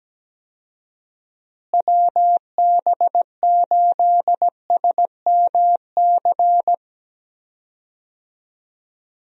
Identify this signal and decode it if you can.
Morse code, heard as a keyed tone, about 17 wpm, 700 Hz. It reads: WB8SMC
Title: USA Call Signs - 6 Characters - Repeated 2x - 17wpm